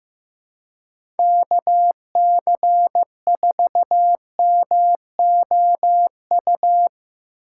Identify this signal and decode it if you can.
Morse code, heard as a keyed tone, about 15 wpm, 700 Hz. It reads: KC4MOU
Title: USA Call Signs - 6 Characters - 4x Character Spacing - 15wpm